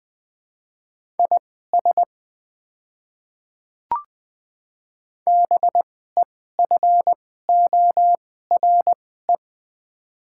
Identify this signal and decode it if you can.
Morse code, heard as a keyed tone, about 20 wpm, spaced as for 14 wpm, 700 Hz. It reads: IS BEFORE
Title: 2x Character Spacing - Top 200 Words - 20wpm